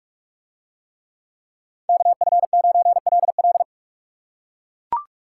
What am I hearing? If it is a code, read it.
Morse code, heard as a keyed tone, about 42 wpm, 700 Hz. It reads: KF9LL